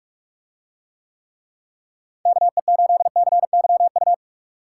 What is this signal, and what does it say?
Morse code, heard as a keyed tone, about 44 wpm, 700 Hz. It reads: KE8CYU